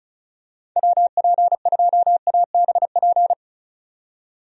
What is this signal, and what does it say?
Morse code, heard as a keyed tone, about 34 wpm, 700 Hz. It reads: WP2ABP
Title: USA Call Signs - 6 Characters - 35wpm